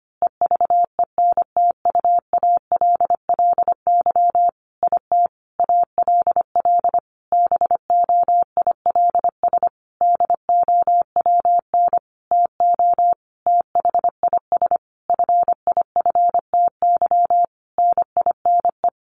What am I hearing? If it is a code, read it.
Morse code, heard as a keyed tone, about 25 wpm, 700 Hz. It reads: EVENTUALLY IT ALL BOILS DOWN TO THIS FIFTY NINE
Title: A Sound Of Thunder 25wpm